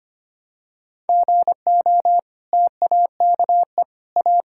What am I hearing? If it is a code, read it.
Morse code, heard as a keyed tone, about 25 wpm, 700 Hz. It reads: GO TAKE A
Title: Sentences from Top 1600 Words 25wpm